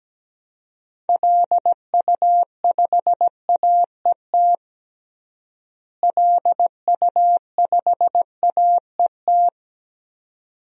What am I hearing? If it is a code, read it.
Morse code, heard as a keyed tone, about 17 wpm, 700 Hz. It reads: LU5AET LU5AET
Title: DXCC-100 Call Signs - 6 Characters - Repeated 2x - 17wpm